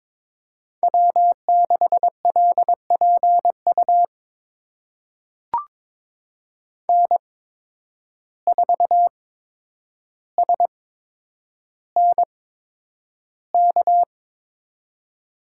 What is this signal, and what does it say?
Morse code, heard as a keyed tone, about 22 wpm, spaced as for 9 wpm, 700 Hz. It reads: W6LPU N4SNK
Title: USA Call Signs - 5 Characters - 8x Character Spacing - 22wpm